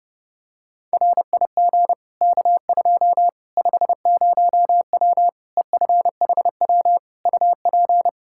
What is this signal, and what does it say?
Morse code, heard as a keyed tone, about 30 wpm, 700 Hz. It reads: RIG K2 50W EFHW UP
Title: Practice QSO - Exchange 2 - Info - 30wpm